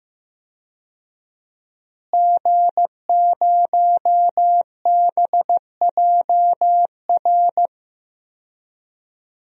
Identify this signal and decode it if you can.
Morse code, heard as a keyed tone, about 15 wpm, 700 Hz. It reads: G0BJR
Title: DXCC-100 Call Signs - 5 Characters - Repeated 2x - 15wpm